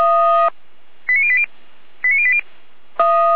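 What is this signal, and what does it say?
Five-tone selective call (ZVEI), heard as quick runs of five tones, tones of about 70 ms. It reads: 89080 89080
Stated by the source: ZVEI five-tone selective call received on an SDR.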